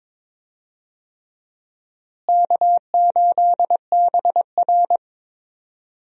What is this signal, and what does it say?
Morse code, heard as a keyed tone, about 22 wpm, 700 Hz. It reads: K8BR
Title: USA Call Signs - 4 Characters - Repeated 3x - 22wpm